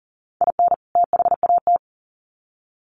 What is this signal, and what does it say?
Morse code, heard as a keyed tone, about 40 wpm, 700 Hz. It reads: IN THAT
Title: Sets of 2 Words 40wpm